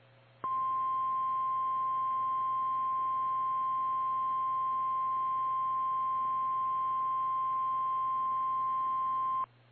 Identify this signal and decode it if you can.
Emergency Alert System (SAME) tones (the 1050 Hz attention tone). It is heard from 0.4 to 9.5 s.
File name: eas / KKPRKXPN / KKPRKXPN